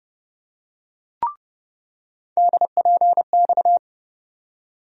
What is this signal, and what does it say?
Morse code, heard as a keyed tone, about 32 wpm, 700 Hz. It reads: DPX
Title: USA Call Sign Postfixes - 3 Letters 30wpm